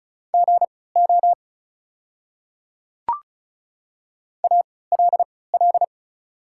Morse code, heard as a keyed tone, about 36 wpm, spaced as for 20 wpm, 690 Hz. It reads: GO ALL
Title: 3x Character Spacing - Top 100 Words - 35wpm